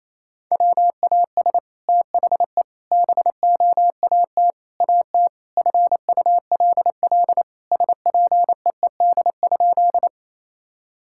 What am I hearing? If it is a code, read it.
Morse code, heard as a keyed tone, about 28 wpm, 700 Hz. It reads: WAS THE BOAT AT FULL SPEED?
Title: Sentences from Top 1000 Words - Review Part 5 - 28wpm